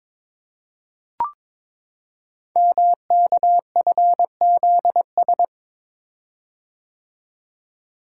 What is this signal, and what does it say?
Morse code, heard as a keyed tone, about 22 wpm, 700 Hz. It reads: MKFZS
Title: Code Groups - 5 Letters - 22wpm